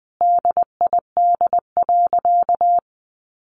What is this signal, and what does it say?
Morse code, heard as a keyed tone, about 20 wpm, 700 Hz. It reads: DID.
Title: Sentences from Top 1400 Words 20wpm